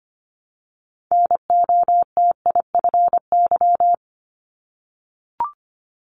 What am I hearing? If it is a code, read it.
Morse code, heard as a keyed tone, about 25 wpm, 700 Hz. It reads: NOTIFY
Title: Multi-letter Phonemes Repeated 3x 25wpm